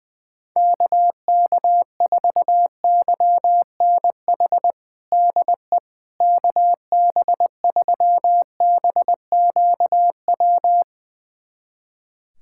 Morse code, heard as a keyed tone, about 20 wpm, 700 Hz. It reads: KK4YNH DE KB3BQW
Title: USA Callsign DE Callsign - 6 Characters - 20wpm